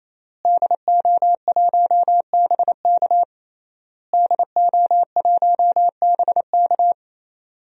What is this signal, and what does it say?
Morse code, heard as a keyed tone, about 28 wpm, 690 Hz. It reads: DO1BK DO1BK
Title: DXCC-100 Call Signs - 5 Characters - Repeated 3x - 28wpm